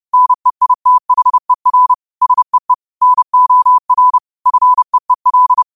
Morse code, heard as a keyed tone, about 30 wpm, 1000 Hz. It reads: NEITHER SEE NOR FEEL
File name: qotdfiles / 30wpm